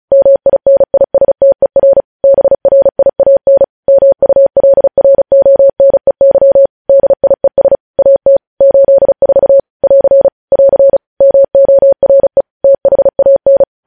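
Morse code, heard as a keyed tone, about 36 wpm, 550 Hz. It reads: MINISTER BRIAN MULRONEY DIES AT 84 <AR> <AR> MORE THAN
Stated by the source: Morse code practice sending news headlines at 35 wpm.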